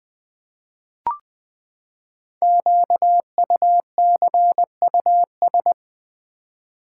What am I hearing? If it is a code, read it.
Morse code, heard as a keyed tone, about 20 wpm, 700 Hz. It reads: QUCUS
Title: Code Groups - 5 Letters - 20wpm